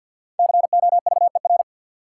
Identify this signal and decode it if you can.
Morse code, heard as a keyed tone, about 53 wpm, 700 Hz. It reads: COVER
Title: Sentences from Top 500 Words - New Word Drill - 50wpm